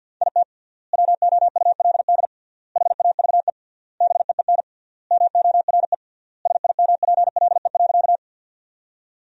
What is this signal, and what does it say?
Morse code, heard as a keyed tone, about 50 wpm, 700 Hz. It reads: IT WOULD HAVE BEEN MORE SIMPLE.
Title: 3x Word-Spacing - Sentences from Top 400 Words - 50wpm